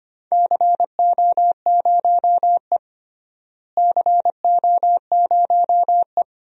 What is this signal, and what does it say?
Morse code, heard as a keyed tone, about 25 wpm, 700 Hz. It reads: CO0E CO0E
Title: DXCC-100 Call Signs - 4 Characters - Repeated 2x - 25wpm